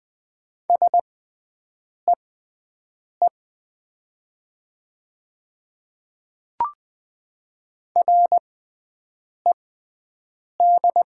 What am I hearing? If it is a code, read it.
Morse code, heard as a keyed tone, about 20 wpm, spaced as for 7 wpm, 700 Hz. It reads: SEE RED
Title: 6x Character Spacing - Top 500 Words - 20wpm